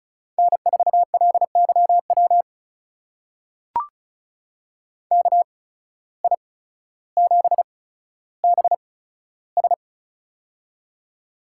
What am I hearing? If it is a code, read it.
Morse code, heard as a keyed tone, about 36 wpm, spaced as for 15 wpm, 700 Hz. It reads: N4LYW KI7BS
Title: USA Call Signs - 8x Character Spacing - 35wpm